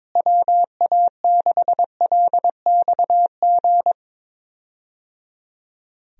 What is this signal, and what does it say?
Morse code, heard as a keyed tone, about 22 wpm, 700 Hz. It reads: WA6LXG
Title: USA Call Signs - 6 Characters - Repeated 3x - 22wpm